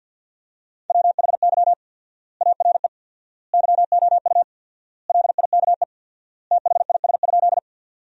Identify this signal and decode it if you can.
Morse code, heard as a keyed tone, about 50 wpm, 700 Hz. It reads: WHY ARE YOU LIKE THIS?
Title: 4x Word-Spacing - Sentences from Top 200 Words - 50wpm